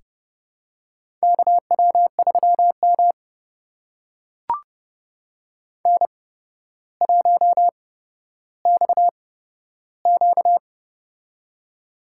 Morse code, heard as a keyed tone, about 30 wpm, spaced as for 13 wpm, 700 Hz. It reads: KW3M N1XQ
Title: USA Call Signs - 4 Characters - 8x Character Spacing - 30wpm